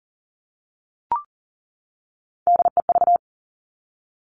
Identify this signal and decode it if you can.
Morse code, heard as a keyed tone, about 42 wpm, 700 Hz. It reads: DEV
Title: Three Challenging Letter-Number Combinations 40wpm